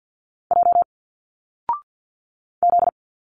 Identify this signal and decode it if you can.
Morse code, heard as a keyed tone, about 50 wpm, 700 Hz. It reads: J Z
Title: Single Letters 50wpm